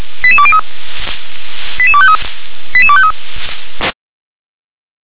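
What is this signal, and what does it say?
Five-tone selective call (ZVEI), heard as quick runs of five tones, tones of about 70 ms. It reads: 88203 88253 88253
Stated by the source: ZVEI five-tone selective call received on SDR.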